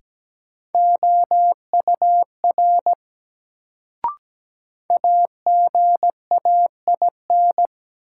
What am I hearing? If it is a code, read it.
Morse code, heard as a keyed tone, about 17 wpm, 700 Hz. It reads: OUR AGAIN
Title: Sentences from Top 200 Words - New Word Drill - 17wpm